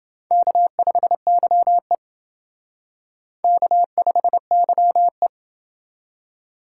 Morse code, heard as a keyed tone, about 29 wpm, 700 Hz. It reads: K5YE K5YE